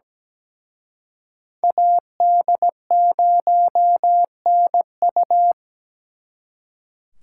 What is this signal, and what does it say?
Morse code, heard as a keyed tone, about 20 wpm, 700 Hz. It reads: AD0NU
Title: Speed Racing - USA 2x2 Call Signs 17wpm